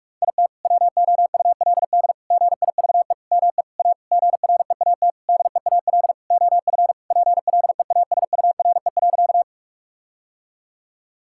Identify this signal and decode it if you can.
Morse code, heard as a keyed tone, about 42 wpm, 700 Hz. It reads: IT WOULD GIVE ME A GREAT DEAL OF PLEASURE.
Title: Sentences from Top 1600 Words 45wpm